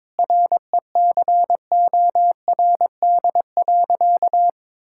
Morse code, heard as a keyed tone, about 21 wpm, 700 Hz. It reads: RECORD.
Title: Sentences from Top 1000 Words - Review Part 6 - 22wpm